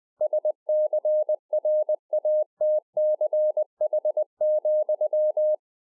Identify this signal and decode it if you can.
Morse code, heard as a keyed tone, about 20 wpm, 610 Hz. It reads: SCRATCH,